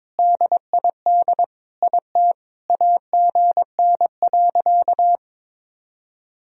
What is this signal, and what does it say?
Morse code, heard as a keyed tone, about 22 wpm, 700 Hz. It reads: DID IT AGN.
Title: Sentences from Top 300 Words and Abbreviations 22wpm